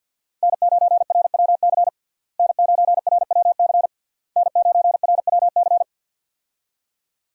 Morse code, heard as a keyed tone, about 46 wpm, 700 Hz. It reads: N9RWC N9RWC N9RWC
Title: USA Call Signs - 5 Characters - Repeated 3x - 50wpm